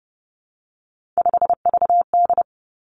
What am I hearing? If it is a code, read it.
Morse code, heard as a keyed tone, about 30 wpm, 700 Hz. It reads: 5VD
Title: Three Challenging Letter-Number Combinations 30wpm